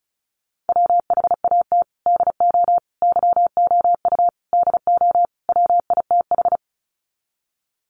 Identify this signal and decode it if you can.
Morse code, heard as a keyed tone, about 34 wpm, 700 Hz. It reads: WHAT DO YOU DO WITH
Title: Sets of 5 Words Encore2 35wpm